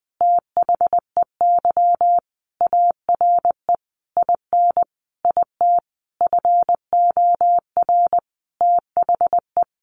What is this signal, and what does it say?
Morse code, heard as a keyed tone, about 20 wpm, 700 Hz. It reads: THEY ARE IN IT FOR THE
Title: Sentences from Top 1300 Words 20wpm